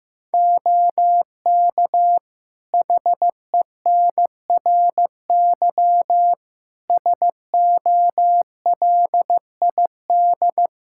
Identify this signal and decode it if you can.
Morse code, heard as a keyed tone, about 15 wpm, 700 Hz. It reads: OK HENRY SOLID
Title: Practice QSO - Exchange 3 - More Info - 15wpm